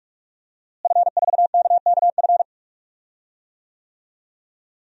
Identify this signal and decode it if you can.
Morse code, heard as a keyed tone, about 46 wpm, 700 Hz. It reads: F4KKF
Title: DXCC-100 Call Signs - 5 Characters - Repeated 2x - 45wpm